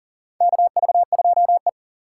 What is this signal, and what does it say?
Morse code, heard as a keyed tone, about 40 wpm, 700 Hz. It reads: KV2E
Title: USA Call Signs - 4 Characters - 6x Character Spacing - 40wpm